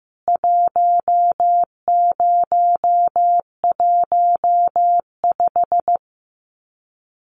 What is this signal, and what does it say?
Morse code, heard as a keyed tone, about 15 wpm, 700 Hz. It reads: 1015
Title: CWT Exchange 15wpm